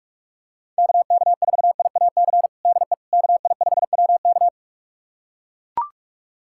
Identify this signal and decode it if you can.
Morse code, heard as a keyed tone, about 42 wpm, 700 Hz. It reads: KK4IAC DE KI5WK
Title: USA Callsign DE Callsign 45wpm